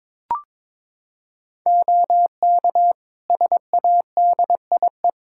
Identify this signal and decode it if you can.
Morse code, heard as a keyed tone, about 21 wpm, 700 Hz. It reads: OK SADIE
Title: Practice QSO - Exchange 3 - More Info - 22wpm